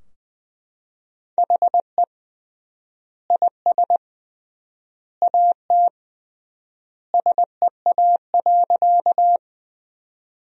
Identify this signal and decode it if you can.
Morse code, heard as a keyed tone, about 20 wpm, 700 Hz. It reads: HE IS AT SEA.
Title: 3x Word-Spacing - Sentences from Top 300 Words - 20wpm